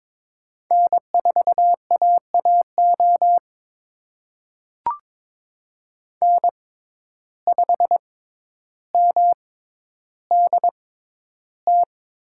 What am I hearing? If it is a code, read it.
Morse code, heard as a keyed tone, about 22 wpm, spaced as for 11 wpm, 700 Hz. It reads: N4AAO N5MDT
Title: USA Call Signs - 5 Characters - 6x Character Spacing - 22wpm